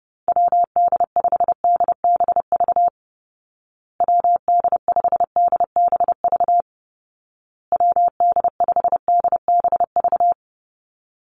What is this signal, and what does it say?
Morse code, heard as a keyed tone, about 30 wpm, 700 Hz. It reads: WD5DBV WD5DBV WD5DBV
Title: Call Signs Repeated 2x - 30wpm